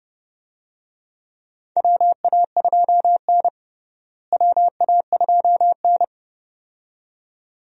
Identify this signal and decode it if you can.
Morse code, heard as a keyed tone, about 32 wpm, 700 Hz. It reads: WA2N WA2N
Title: USA Call Signs - 4 Characters - Repeated 2x - 30wpm